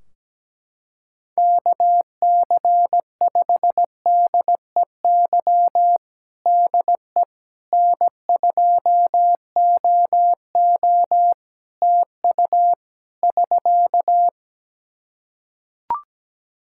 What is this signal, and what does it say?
Morse code, heard as a keyed tone, about 17 wpm, 700 Hz. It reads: KC5DEY DE N2OO TU <SK>
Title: K1USN SST QSO Practice 17wpm